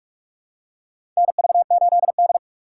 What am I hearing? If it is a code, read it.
Morse code, heard as a keyed tone, about 42 wpm, 700 Hz. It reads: NV8D
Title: USA Call Signs - 4x Character Spacing - 45wpm